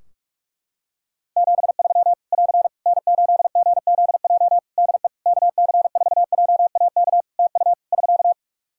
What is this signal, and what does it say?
Morse code, heard as a keyed tone, about 46 wpm, 700 Hz. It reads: 73 <AR> N8GZJ DE KC4JAK TU <SK>
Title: Practice QSO - Exchange 4 - Ending - 45wpm